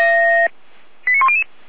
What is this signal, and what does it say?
Five-tone selective call (ZVEI), heard as quick runs of five tones, tones of about 70 ms. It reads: 89100